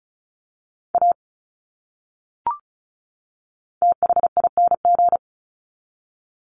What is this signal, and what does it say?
Morse code, heard as a keyed tone, about 34 wpm, 700 Hz. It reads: A THING
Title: Top 200 Words 35wpm